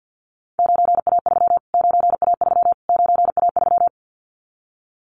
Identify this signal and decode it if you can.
Morse code, heard as a keyed tone, about 50 wpm, 700 Hz. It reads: 9A3 9A3 9A3
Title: Callsign Prefixes for Top 25 DXCC - Repeated 3x - 50wpm